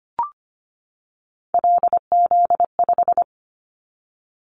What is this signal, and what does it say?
Morse code, heard as a keyed tone, about 25 wpm, 700 Hz. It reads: LZ5
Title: Callsign Prefixes for Top 100 DXCC - 25wpm